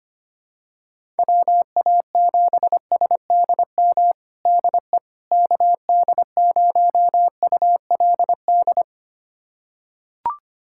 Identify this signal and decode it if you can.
Morse code, heard as a keyed tone, about 25 wpm, 700 Hz. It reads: WA7SDM DE KD0ULD
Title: USA Callsign DE Callsign - 6 Characters - 25wpm